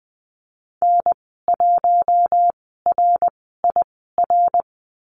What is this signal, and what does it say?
Morse code, heard as a keyed tone, about 20 wpm, spaced as for 15 wpm, 700 Hz. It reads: N1RIR
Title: 062 Lesson 11g - Letter h - Call Signs 20wpm